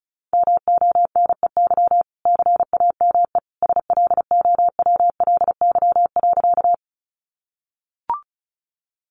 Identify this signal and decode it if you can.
Morse code, heard as a keyed tone, about 34 wpm, 700 Hz. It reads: MONEY CAME SLOWLY.
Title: Sentences from Top 1100 Words 35wpm